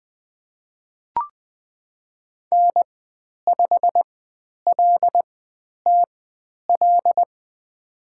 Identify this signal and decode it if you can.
Morse code, heard as a keyed tone, about 20 wpm, spaced as for 10 wpm, 700 Hz. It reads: N5LTL
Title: 102 Lesson 17g - Letter u - Call Signs 20wpm